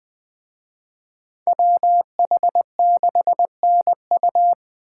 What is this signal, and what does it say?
Morse code, heard as a keyed tone, about 20 wpm, 700 Hz. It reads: WH6NU